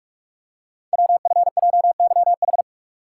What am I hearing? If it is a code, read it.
Morse code, heard as a keyed tone, about 44 wpm, 700 Hz. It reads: WFJYH
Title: Code Groups - 5 Letters - 45wpm